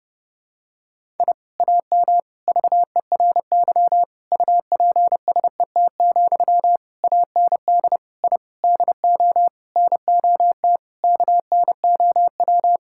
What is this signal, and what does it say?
Morse code, heard as a keyed tone, about 30 wpm, 700 Hz. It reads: I AM VERY UPSET, AND I DO NOT KNOW
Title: Sentences from Top 1000 Words - Review Part 5 - 30wpm